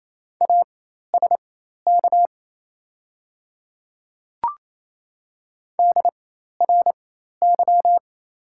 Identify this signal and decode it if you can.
Morse code, heard as a keyed tone, about 28 wpm, spaced as for 13 wpm, 700 Hz. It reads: ASK DRY
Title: 4x Character Spacing - Top 500 Words - 28wpm